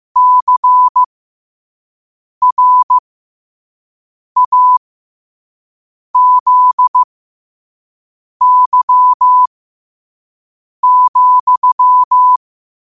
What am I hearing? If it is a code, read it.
Morse code, heard as a keyed tone, about 15 wpm, spaced as for 5 wpm, 1000 Hz. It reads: CRAZY,
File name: qotdfiles / 5wpm